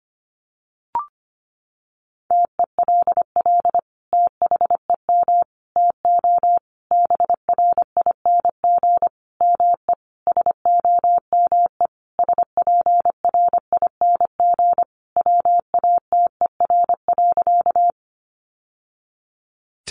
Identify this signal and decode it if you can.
Morse code, heard as a keyed tone, about 25 wpm, 700 Hz. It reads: TELL THEM TO BRING ME SOME SPRING WATER.
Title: Sentences from Top 1200 Words 25wpm